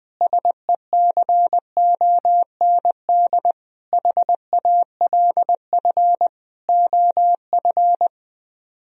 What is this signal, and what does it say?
Morse code, heard as a keyed tone, about 20 wpm, 700 Hz. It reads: SECOND HALF OF
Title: Speed-Racing - Sets of 5 Words 17wpm